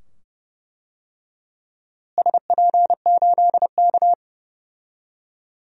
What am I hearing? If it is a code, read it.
Morse code, heard as a keyed tone, about 31 wpm, 700 Hz. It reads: SP8K
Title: DXCC-100 Call Signs - 4 Characters - Repeated 2x - 30wpm